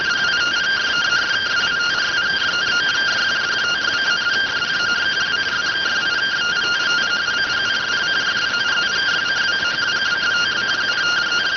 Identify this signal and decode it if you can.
Two-tone teleprinter signal, about 170 Hz shift.